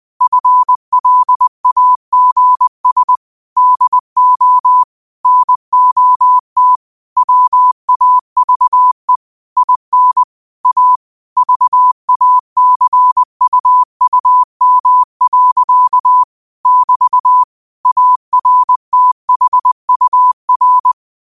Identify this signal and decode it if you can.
Morse code, heard as a keyed tone, about 20 wpm, 1000 Hz. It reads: FLAGS DO NOT WAVE IN A VACUUM. <BT> ARTHUR